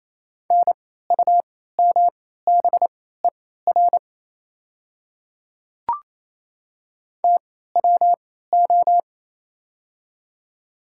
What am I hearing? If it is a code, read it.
Morse code, heard as a keyed tone, about 28 wpm, spaced as for 16 wpm, 700 Hz. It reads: NUMBER TWO
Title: 3x Character Spacing - Top 100 Words - 28wpm